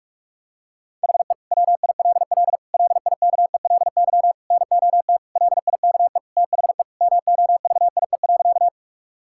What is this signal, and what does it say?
Morse code, heard as a keyed tone, about 45 wpm, 700 Hz. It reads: HE WILL LIKELY NOT LIKE THE MOVIE.
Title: Sentences from Top 1000 Words - Review Part 5 - 45wpm